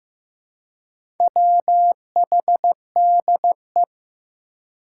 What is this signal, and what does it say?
Morse code, heard as a keyed tone, about 15 wpm, 700 Hz. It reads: WHDE
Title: Four Challenging Letter-Number Combinations 15wpm